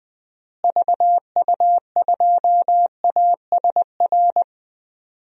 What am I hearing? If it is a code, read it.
Morse code, heard as a keyed tone, about 20 wpm, 700 Hz. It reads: VU2ASR
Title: DXCC-100 Call Signs - 6 Characters - Repeated 2x - 20wpm